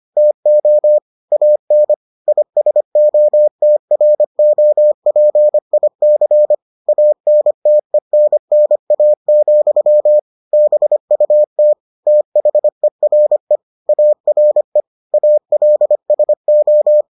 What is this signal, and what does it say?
Morse code, heard as a keyed tone, about 25 wpm, 600 Hz. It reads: TO AN ISOTROPIC ANTENNA, BUT THERE ARE ALSO